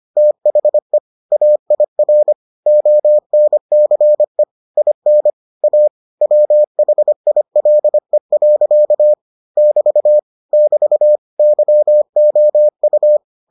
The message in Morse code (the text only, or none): THE AIR ONCE IN A WHILE. <BT> <BT> YOU